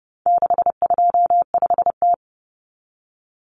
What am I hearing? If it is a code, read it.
Morse code, heard as a keyed tone, about 30 wpm, 700 Hz. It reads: 625T
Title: Common Cut Numbers - 4-Digit 30wpm